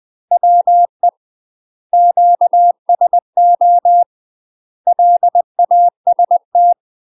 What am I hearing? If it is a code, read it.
Morse code, heard as a keyed tone, about 20 wpm, 700 Hz. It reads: WE QSO LAST
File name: UK MASTER QSOs TRAINING / 20WPM / UK MASTER QSOs 0044